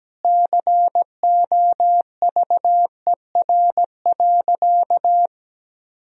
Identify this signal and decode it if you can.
Morse code, heard as a keyed tone, about 17 wpm, 700 Hz. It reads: COVER.
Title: Sentences from Top 1000 Words - Review Part 1 - 17wpm